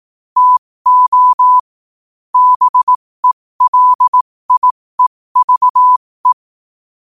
Morse code, heard as a keyed tone, about 18 wpm, spaced as for 15 wpm, 1000 Hz. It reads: TO BELIEVE
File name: qotdfiles / 15wpm